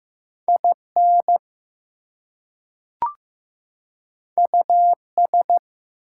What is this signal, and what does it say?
Morse code, heard as a keyed tone, about 15 wpm, 700 Hz. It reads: IN US
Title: Two Letter Words - 15wpm